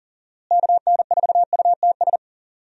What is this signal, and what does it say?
Morse code, heard as a keyed tone, about 40 wpm, 700 Hz. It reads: KN4UTS
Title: USA Call Signs - 6 Characters - Repeated 3x - 40wpm